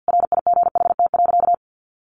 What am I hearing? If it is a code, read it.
Morse code, heard as a keyed tone, about 50 wpm, 700 Hz. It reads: RIGHT.
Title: Sentences from Top 1100 Words 50wpm